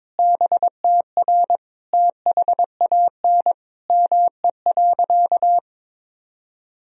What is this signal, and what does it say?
Morse code, heard as a keyed tone, about 22 wpm, 700 Hz. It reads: BTR THAN ME.
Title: Sentences from Top 400 Words and Abbreviations 22wpm